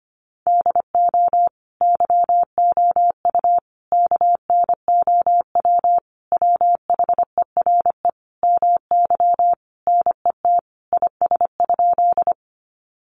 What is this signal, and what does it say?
Morse code, heard as a keyed tone, about 25 wpm, 700 Hz. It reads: DO YOU KNOW WHERE MY NET IS?
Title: Sentences from Top 1400 Words 25wpm